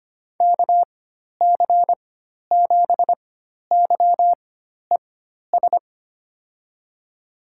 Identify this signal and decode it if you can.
Morse code, heard as a keyed tone, about 25 wpm, spaced as for 12 wpm, 700 Hz. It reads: KC7YES
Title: USA Call Signs - 6 Characters - 4x Character Spacing - 25wpm